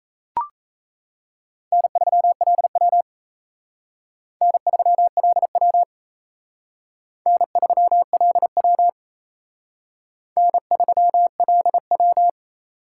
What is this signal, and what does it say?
Morse code, heard as a keyed tone, about 34 wpm, 700 Hz. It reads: N3LW N3LW N3LW N3LW
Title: Speed Racing - USA 1x2 Call Signs 28wpm